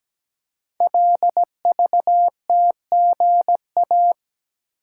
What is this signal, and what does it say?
Morse code, heard as a keyed tone, about 17 wpm, 700 Hz. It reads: LVTGA
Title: Code Groups - 5 Letters - 17wpm